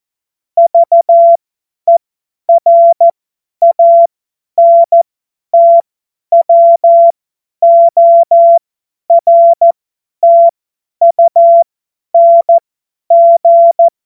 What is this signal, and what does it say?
Morse code, heard as a keyed tone, about 14 wpm, spaced as for 10 wpm, 680 Hz. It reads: VERANTWORTUNG
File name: Morsekurs / Witze / Witz 23. Dezember 2025 50 F70